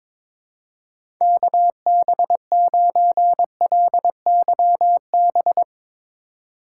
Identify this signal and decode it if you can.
Morse code, heard as a keyed tone, about 22 wpm, 700 Hz. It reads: KB9LYB
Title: USA Call Signs - 6 Characters - 3x Character Spacing - 22wpm